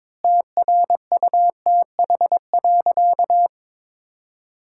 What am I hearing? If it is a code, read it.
Morse code, heard as a keyed tone, about 22 wpm, 700 Hz. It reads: TRUTH.
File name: Sentences with 9-Word Phrases - 22wpm